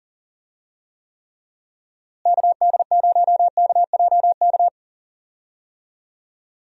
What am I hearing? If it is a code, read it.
Morse code, heard as a keyed tone, about 40 wpm, 700 Hz. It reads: KD0KJK
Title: USA Call Signs - 6 Characters - Repeated 3x - 40wpm